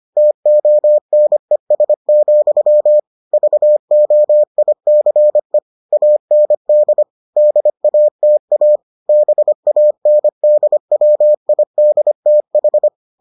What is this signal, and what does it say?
Morse code, heard as a keyed tone, about 25 wpm, 600 Hz. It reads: TONES, VOICE AND DATA BANDWIDTH